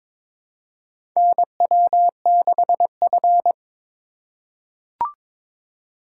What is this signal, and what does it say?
Morse code, heard as a keyed tone, about 21 wpm, 700 Hz. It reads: NW6F
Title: USA Call Signs - 4x Character Spacing - 22wpm